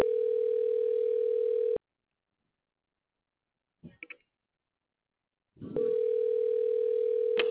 Ringback tone (North America), 440 + 480 Hz.